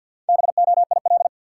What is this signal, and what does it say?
Morse code, heard as a keyed tone, about 52 wpm, 700 Hz. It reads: BOIL
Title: Sets of 5 Words Encore - New Word Drill - 50wpm